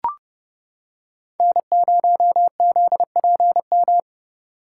Morse code, heard as a keyed tone, about 30 wpm, 700 Hz. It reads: N0ZPM